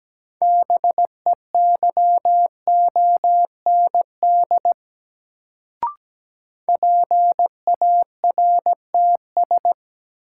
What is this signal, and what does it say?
Morse code, heard as a keyed tone, about 17 wpm, 700 Hz. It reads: BEYOND PARTS
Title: Sentences from Top 1000 Words - Review Part 6 - New Word Drill - 17wpm